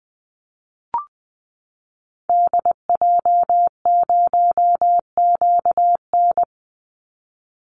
Morse code, heard as a keyed tone, about 20 wpm, 700 Hz. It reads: DJ0QN
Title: Call Signs Repeated 2x - 20wpm